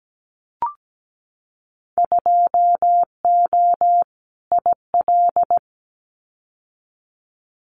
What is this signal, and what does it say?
Morse code, heard as a keyed tone, about 17 wpm, 700 Hz. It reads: 2O IL
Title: Winter Field Day Exchanges 17wpm